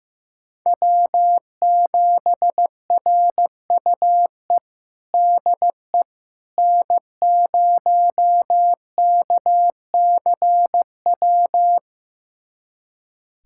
Morse code, heard as a keyed tone, about 15 wpm, 700 Hz. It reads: W7RUE DE N0KCW
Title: USA Callsign DE Callsign - 5 Characters - 15wpm